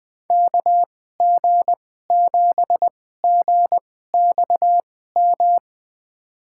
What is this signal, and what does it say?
Morse code, heard as a keyed tone, about 20 wpm, spaced as for 15 wpm, 700 Hz. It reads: KG7GXM
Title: USA Call Signs - 6 Characters - 2x Character Spacing - 20wpm